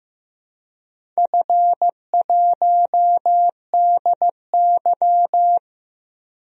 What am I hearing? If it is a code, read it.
Morse code, heard as a keyed tone, about 15 wpm, 700 Hz. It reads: F1DY
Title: DXCC-100 Call Signs - 4 Characters - 15wpm